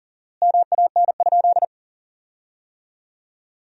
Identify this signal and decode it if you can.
Morse code, heard as a keyed tone, about 41 wpm, 700 Hz. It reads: MAN?
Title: 4x Word-Spacing - Sentences from Top 200 Words - 40wpm